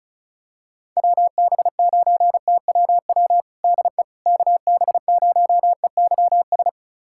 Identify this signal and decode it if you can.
Morse code, heard as a keyed tone, about 34 wpm, 700 Hz. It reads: WB9TWW DE KB0EYS
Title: USA Callsign DE Callsign - 6 Characters - 35wpm